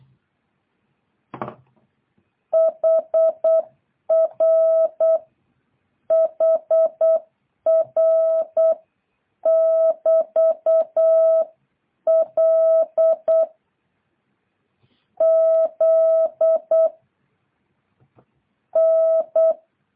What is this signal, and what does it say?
Morse code, heard as a keyed tone, about 8 wpm, 660 Hz. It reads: HR HR <BT> L Z N